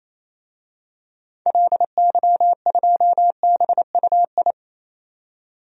Morse code, heard as a keyed tone, about 28 wpm, 700 Hz. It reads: LY2BUI
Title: DXCC-100 Call Signs - 6 Characters - 28wpm